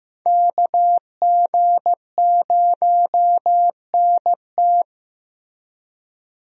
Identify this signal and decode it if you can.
Morse code, heard as a keyed tone, about 16 wpm, 700 Hz. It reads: KG0NT
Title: USA Call Signs - 5 Characters - Repeated 2x - 15wpm